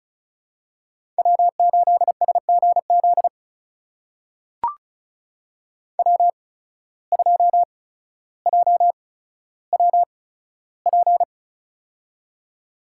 Morse code, heard as a keyed tone, about 34 wpm, spaced as for 15 wpm, 690 Hz. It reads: W8SGZ W2JWP
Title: USA Call Signs - 5 Characters - 8x Character Spacing - 35wpm